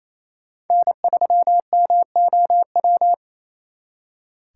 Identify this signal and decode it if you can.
Morse code, heard as a keyed tone, about 35 wpm, 700 Hz. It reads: N3MOW
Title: Speed Racing - USA 1x3 Call Signs 28wpm